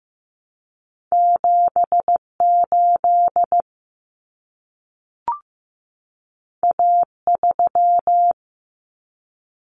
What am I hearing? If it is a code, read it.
Morse code, heard as a keyed tone, about 15 wpm, 700 Hz. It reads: 78 A3
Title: Common Cut Numbers - 2-Digit 15wpm